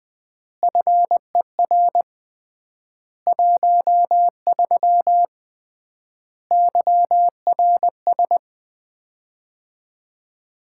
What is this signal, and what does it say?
Morse code, heard as a keyed tone, about 20 wpm, 700 Hz. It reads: FER 13 YRS
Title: Common QSO Phrases - 3x Word-Spacing - 20wpm